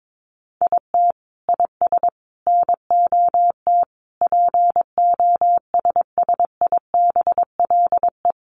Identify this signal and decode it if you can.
Morse code, heard as a keyed tone, about 22 wpm, 700 Hz. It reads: IT IS NOT POSSIBLE
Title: Sets of 5 Words Encore2 22wpm